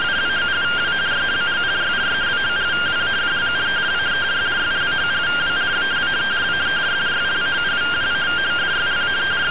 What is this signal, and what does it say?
Two-tone teleprinter signal, about 170 Hz shift.
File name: findings / SAGITTAL 43mm 4mm.gen-03 rtty rx